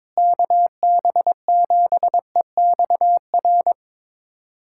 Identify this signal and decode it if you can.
Morse code, heard as a keyed tone, about 22 wpm, 700 Hz. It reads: KB7EXR
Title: USA Callsign DE Callsign 22wpm